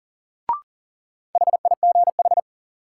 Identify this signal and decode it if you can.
Morse code, heard as a keyed tone, about 40 wpm, 700 Hz. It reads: HIGH
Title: Sentences from Top 200 Words - New Word Drill - 40wpm